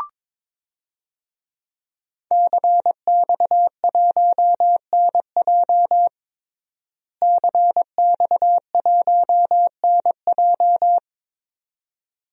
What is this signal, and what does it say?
Morse code, heard as a keyed tone, about 22 wpm, 700 Hz. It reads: CX1NJ CX1NJ
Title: DXCC-100 Call Signs - 5 Characters - Repeated 2x - 22wpm